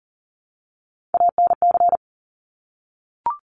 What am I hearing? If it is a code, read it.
Morse code, heard as a keyed tone, about 40 wpm, 700 Hz. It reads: ANC
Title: Top Three Letter Combinations in Words 40wpm